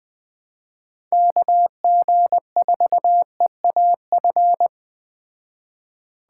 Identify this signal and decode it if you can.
Morse code, heard as a keyed tone, about 20 wpm, 700 Hz. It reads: KG4EAF
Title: Speed Racing - USA Call Signs 17wpm